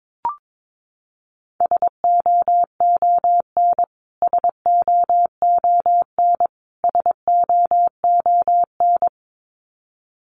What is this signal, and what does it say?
Morse code, heard as a keyed tone, about 22 wpm, 700 Hz. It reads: SOON SOON SOON
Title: Top 500 Words Repeated 22wpm